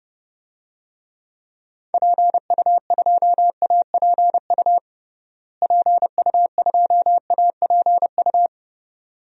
Morse code, heard as a keyed tone, about 30 wpm, 700 Hz. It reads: PU2APU PU2APU
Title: DXCC-100 Call Signs - 6 Characters - Repeated 2x - 30wpm